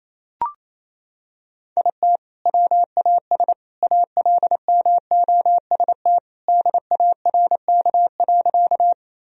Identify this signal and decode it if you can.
Morse code, heard as a keyed tone, about 28 wpm, 700 Hz. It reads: IT WAS ALMOST DARK.
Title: Sentences from Top 1000 Words - Review Part 4 - 28wpm